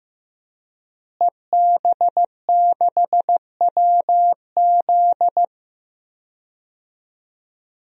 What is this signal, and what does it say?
Morse code, heard as a keyed tone, about 15 wpm, 700 Hz. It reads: EB6WZ
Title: DXCC-100 Call Signs - 5 Characters - Repeated 3x - 15wpm